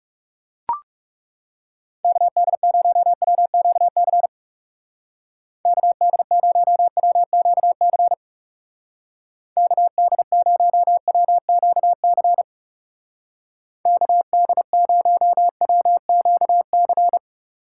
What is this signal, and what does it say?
Morse code, heard as a keyed tone, about 36 wpm, 700 Hz. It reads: KD0WQC KD0WQC KD0WQC KD0WQC
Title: Speed Racing - USA 2x3 Call Signs 30wpm